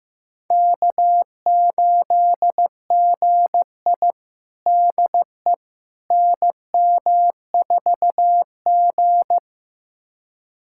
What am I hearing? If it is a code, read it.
Morse code, heard as a keyed tone, about 15 wpm, 700 Hz. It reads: K8GI DE NM4G
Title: USA Callsign DE Callsign - 4 Characters - 15wpm